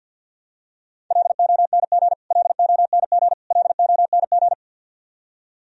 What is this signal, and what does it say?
Morse code, heard as a keyed tone, about 48 wpm, 700 Hz. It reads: LONG LONG LONG
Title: Top 100 Words Repeated 50wpm